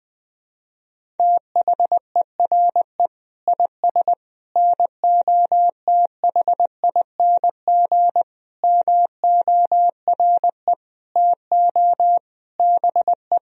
Morse code, heard as a keyed tone, about 20 wpm, 700 Hz. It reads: THERE IS NOTHING MORE TO BE
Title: Sentences from Top 500 Words and Abbreviations 20wpm